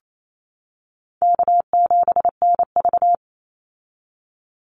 Morse code, heard as a keyed tone, about 28 wpm, 700 Hz. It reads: K7NV
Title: Easy Call Signs - 28wpm